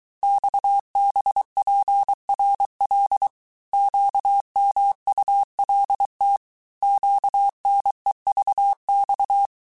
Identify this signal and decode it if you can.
Morse code, heard as a keyed tone, about 23 wpm, 800 Hz. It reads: XBPRL QMULT QNEVX
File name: cours-du-mercredi / audios / cours-855-texte-chiffre-vitesse-22-mpm